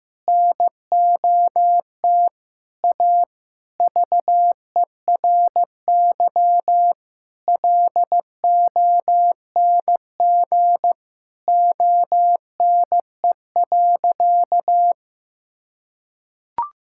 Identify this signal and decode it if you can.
Morse code, heard as a keyed tone, about 15 wpm, 700 Hz. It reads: NOT A VERY LONG ONE.
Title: Sentences from Top 1000 Words - Review Part 5 - 15wpm